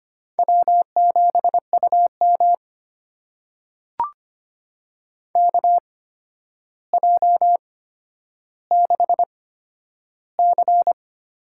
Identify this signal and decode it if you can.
Morse code, heard as a keyed tone, about 25 wpm, spaced as for 11 wpm, 700 Hz. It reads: W7UM KJ6C
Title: USA Call Signs - 8x Character Spacing - 25wpm